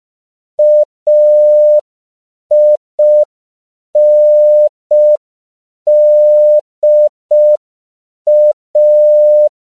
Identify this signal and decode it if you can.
Morse code, heard as a keyed tone, about 5 wpm, 600 Hz. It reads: AINDA
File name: 5PPM